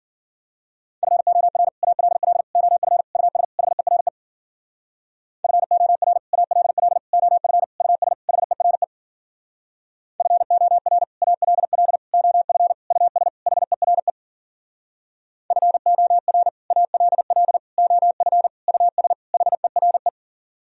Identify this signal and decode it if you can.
Morse code, heard as a keyed tone, about 54 wpm, 700 Hz. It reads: FOR ALL OF US HERE FOR ALL OF US HERE FOR ALL OF US HERE FOR ALL OF US HERE